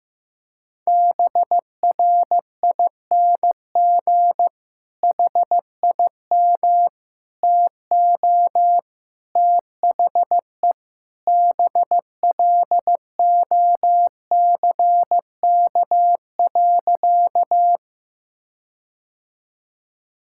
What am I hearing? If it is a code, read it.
Morse code, heard as a keyed tone, about 15 wpm, 700 Hz. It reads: BRING HIM TO THE BLOCK.
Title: Sentences from Top 1600 Words 15wpm